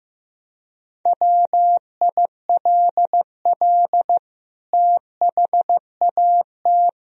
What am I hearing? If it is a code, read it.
Morse code, heard as a keyed tone, about 16 wpm, 700 Hz. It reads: WILL THAT
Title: Sentences from Top 1000 Words - Review Part 2 - 15wpm